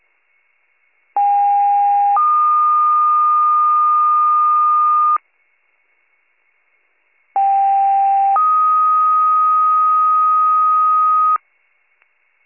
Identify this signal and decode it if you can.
Two-tone page, heard as a one-second tone, then a three-second tone, about 810 then 1220 Hz, about 790 then 1250 Hz. It is heard from 1.2 to 11.4 s.